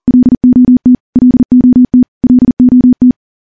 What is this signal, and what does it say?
Morse code, heard as a keyed tone, about 41 wpm, 250 Hz. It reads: LOT LOT LOT
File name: morse / morse-training-40WPM